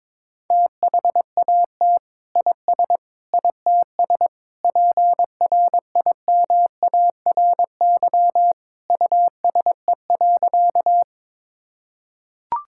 Morse code, heard as a keyed tone, about 22 wpm, 700 Hz. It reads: THAT IS ITS PRIMARY USE.
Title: Sentences from Top 1000 Words - Review Part 5 - 22wpm